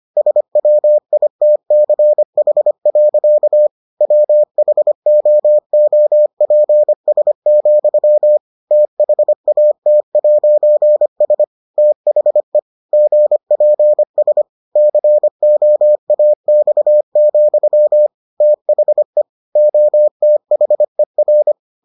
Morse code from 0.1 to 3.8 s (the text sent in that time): SWITCH.